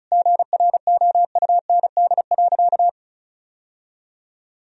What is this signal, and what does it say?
Morse code, heard as a keyed tone, about 36 wpm, 700 Hz. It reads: GROUND.
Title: Sentences from Top 1000 Words - Review Part 1 - 35wpm